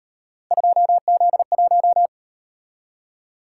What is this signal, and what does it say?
Morse code, heard as a keyed tone, about 37 wpm, 700 Hz. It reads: 2Z1